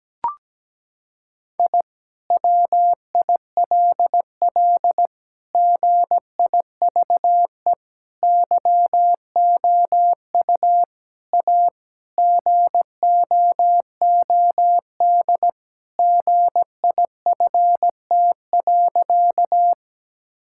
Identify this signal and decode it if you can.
Morse code, heard as a keyed tone, about 17 wpm, 700 Hz. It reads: I WILL GIVE YOU A GOOD GIFT.
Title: Sentences from Top 1000 Words - Review Part 6 - 17wpm